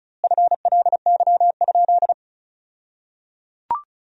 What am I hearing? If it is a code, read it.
Morse code, heard as a keyed tone, about 36 wpm, 700 Hz. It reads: FLY?
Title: 2x Word-Spacing - Sentences from Top 500 Words - 35wpm